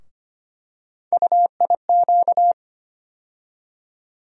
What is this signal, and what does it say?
Morse code, heard as a keyed tone, about 25 wpm, 700 Hz. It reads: UIQ
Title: USA Call Sign Postfixes - 3 Letters 25wpm